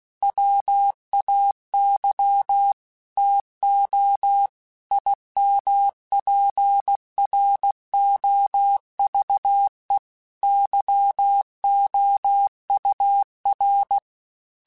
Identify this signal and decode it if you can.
Morse code, heard as a keyed tone, about 16 wpm, 800 Hz. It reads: WAY TO IMPROVE YOUR
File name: Quotes / Quotes-mp3s / A Smile @15